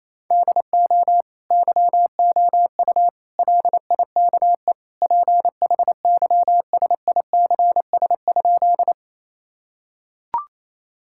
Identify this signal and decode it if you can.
Morse code, heard as a keyed tone, about 27 wpm, 700 Hz. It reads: DO YOU LIKE PHYSICS?
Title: Sentences from Top 1000 Words - Review Part 5 - 28wpm